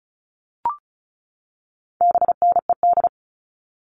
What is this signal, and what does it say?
Morse code, heard as a keyed tone, about 34 wpm, 700 Hz. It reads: BNED
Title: All Cut Numbers - 4-Digit 35wpm